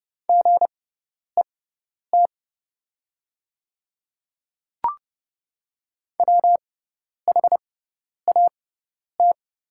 Morse code, heard as a keyed tone, about 31 wpm, spaced as for 10 wpm, 690 Hz. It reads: GET WHAT
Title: 6x Character Spacing - Top 200 Words - 30wpm